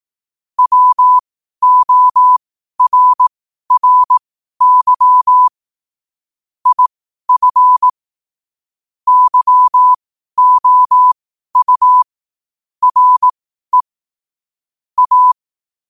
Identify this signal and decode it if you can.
Morse code, heard as a keyed tone, about 18 wpm, spaced as for 12 wpm, 1000 Hz. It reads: WORRY IF YOU RE A